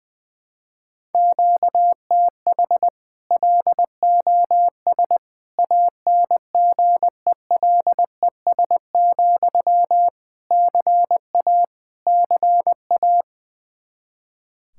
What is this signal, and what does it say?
Morse code, heard as a keyed tone, about 20 wpm, 700 Hz. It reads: QTH LOS ANGELES, CA CA
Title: Practice QSO - Exchange 1 - Beginning - 20wpm